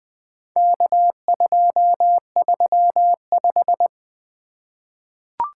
Morse code, heard as a keyed tone, about 20 wpm, 700 Hz. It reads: K235
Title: Four Letter-Number Combinations - 20wpm